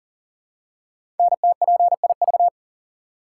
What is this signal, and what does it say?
Morse code, heard as a keyed tone, about 40 wpm, 700 Hz. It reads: NTPIV
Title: Code Groups - 5 Letter:Numbers - 40wpm